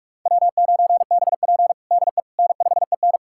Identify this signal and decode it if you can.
Morse code, heard as a keyed tone, about 46 wpm, 700 Hz. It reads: W9BP DE N5EN
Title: USA Callsign DE Callsign - 4 Characters - 45wpm